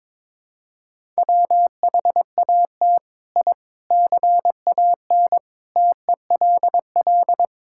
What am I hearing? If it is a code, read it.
Morse code, heard as a keyed tone, about 22 wpm, 700 Hz. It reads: WHAT I CAN TELL
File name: Sentences with 7-Word Phrases - 22wpm